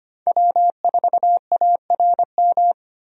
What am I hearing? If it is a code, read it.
Morse code, heard as a keyed tone, about 25 wpm, 700 Hz. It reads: W4ARM
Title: USA Call Signs - 5 Characters - 25wpm